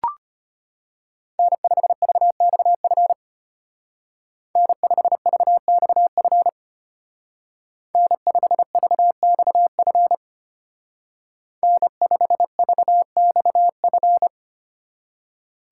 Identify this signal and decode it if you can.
Morse code, heard as a keyed tone, about 33 wpm, 700 Hz. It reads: N5VXF N5VXF N5VXF N5VXF